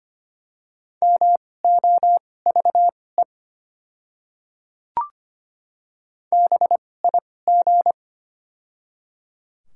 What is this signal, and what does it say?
Morse code, heard as a keyed tone, about 25 wpm, spaced as for 18 wpm, 700 Hz. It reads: MOVE BIG
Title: 2x Character Spacing - Top 200 Words - 25wpm